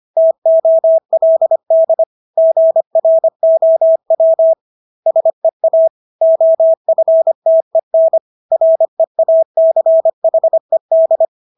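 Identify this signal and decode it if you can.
Morse code, heard as a keyed tone, about 25 wpm, 650 Hz. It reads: TOLD GROW SEA OFTEN REACHED
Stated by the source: Morse practice of English words at 25 wpm from CW med Gnister.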